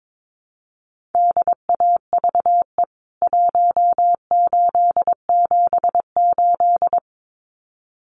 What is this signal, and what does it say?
Morse code, heard as a keyed tone, about 22 wpm, 700 Hz. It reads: DAVE 1878
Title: CWT Exchange 22wpm